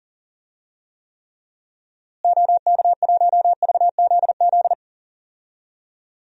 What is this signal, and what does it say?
Morse code, heard as a keyed tone, about 40 wpm, 700 Hz. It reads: OK1VZZ
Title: DXCC-100 Call Signs - 6 Characters - Repeated 2x - 40wpm